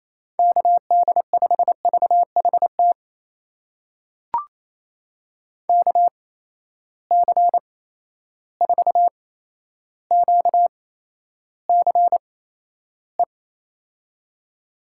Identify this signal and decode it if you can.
Morse code, heard as a keyed tone, about 28 wpm, spaced as for 12 wpm, 700 Hz. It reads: KD5VHT KC4QCE
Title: USA Call Signs - 6 Characters - 8x Character Spacing - 28wpm